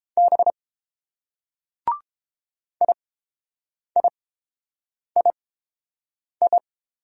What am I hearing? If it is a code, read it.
Morse code, heard as a keyed tone, about 30 wpm, spaced as for 7 wpm, 700 Hz. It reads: B IIII